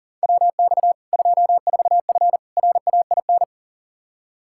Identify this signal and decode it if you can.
Morse code, heard as a keyed tone, about 40 wpm, 700 Hz. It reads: WX 24F RAIN
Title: Practice QSO - Exchange 2 - Info - 40wpm